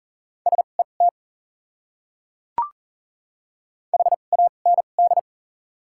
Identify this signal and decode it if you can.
Morse code, heard as a keyed tone, about 37 wpm, spaced as for 28 wpm, 700 Hz. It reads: SET HAND
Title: 2x Character Spacing - Top 200 Words - 40wpm